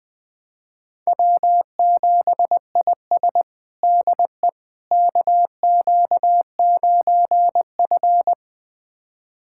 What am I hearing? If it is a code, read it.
Morse code, heard as a keyed tone, about 20 wpm, 700 Hz. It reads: W7IS DE KQ9F
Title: USA Callsign DE Callsign - 4 Characters - 20wpm